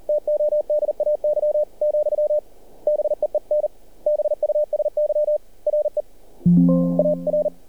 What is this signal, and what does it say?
Morse code, heard as a keyed tone, about 38 wpm, 600 Hz. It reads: TODAY , BEEN BUSY RE AR
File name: archived / CRSnet